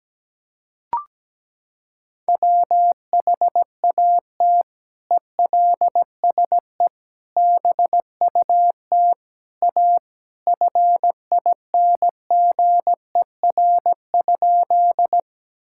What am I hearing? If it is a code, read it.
Morse code, heard as a keyed tone, about 17 wpm, 700 Hz. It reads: WHAT ELSE BUT A FINGER?
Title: Sentences from Top 1600 Words 17wpm